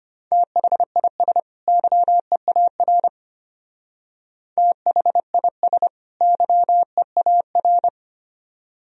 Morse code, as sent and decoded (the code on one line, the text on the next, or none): - .... .. ... / -.-- . .- .-. / - .... .. ... / -.-- . .- .-.
THIS YEAR THIS YEAR